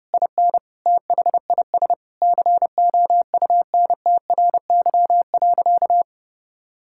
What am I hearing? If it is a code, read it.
Morse code, heard as a keyed tone, about 30 wpm, 700 Hz. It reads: IN THIS COUNTRY.
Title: Sentences from Top 1000 Words - Review Part 5 - 30wpm